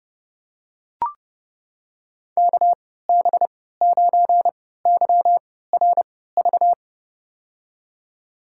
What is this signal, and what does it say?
Morse code, heard as a keyed tone, about 32 wpm, spaced as for 17 wpm, 700 Hz. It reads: KB9YRV
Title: USA Call Signs - 3x Character Spacing - 30wpm